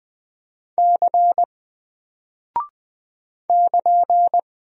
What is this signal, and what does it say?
Morse code, heard as a keyed tone, about 20 wpm, 700 Hz. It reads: C <KN>
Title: Single Character (Letter:Number:Punctuation:Prosign) 20wpm